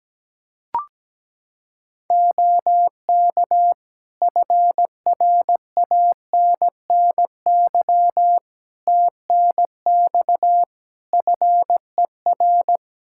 Morse code, heard as a keyed tone, about 17 wpm, 700 Hz. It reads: OK FRANNY TNX FER
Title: Practice QSO - Exchange 4 - Ending - 17wpm